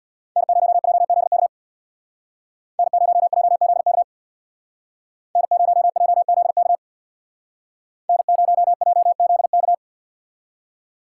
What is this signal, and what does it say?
Morse code, heard as a keyed tone, about 64 wpm, 700 Hz. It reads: N0JZK N0JZK N0JZK N0JZK